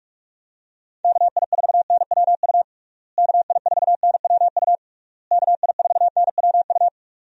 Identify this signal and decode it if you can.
Morse code, heard as a keyed tone, about 43 wpm, 700 Hz. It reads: KI4NWU KI4NWU KI4NWU
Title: USA Call Signs - 6 Characters - Repeated 3x - 45wpm